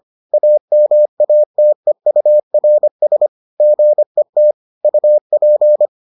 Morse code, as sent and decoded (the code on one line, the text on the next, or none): .- -- .- - . ..- .-. ... / --. - / ..- .--.
AMATEURS GET UP